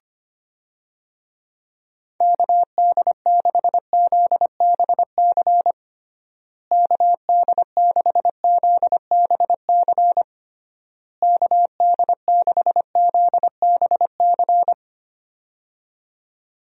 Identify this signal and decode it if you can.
Morse code, heard as a keyed tone, about 25 wpm, 700 Hz. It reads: KD6ZBC KD6ZBC KD6ZBC
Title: USA Call Signs - 6 Characters - Repeated 3x - 25wpm